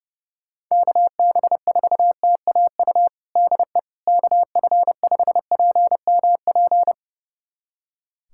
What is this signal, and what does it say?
Morse code, heard as a keyed tone, about 30 wpm, 700 Hz. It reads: KB4TAU DE KF5PMP
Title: USA Callsign DE Callsign - 6 Characters - 30wpm